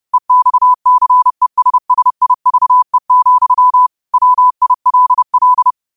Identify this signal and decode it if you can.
Morse code, heard as a keyed tone, about 30 wpm, 1000 Hz. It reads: EXCESSIVE, WILL
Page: Quotes for Mon, 28 Apr 2025 in Morse Code at 30 words per minute.